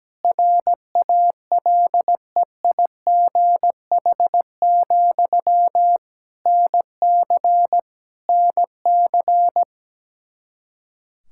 Morse code, heard as a keyed tone, about 17 wpm, 700 Hz. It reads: RALEIGH, NC NC
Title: K1USN SST QSO Practice 17wpm